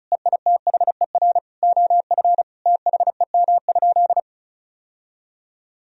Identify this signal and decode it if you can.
Morse code, heard as a keyed tone, about 36 wpm, 700 Hz. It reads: EITHER OF THEM?
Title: Sentences from Top 1000 Words - Review Part 4 - 35wpm